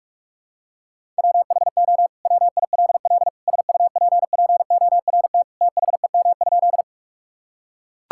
Morse code, heard as a keyed tone, about 41 wpm, 700 Hz. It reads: WHO WILL SUPPORT THEM?
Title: Sentences from Top 1000 Words - Review Part 5 - 45wpm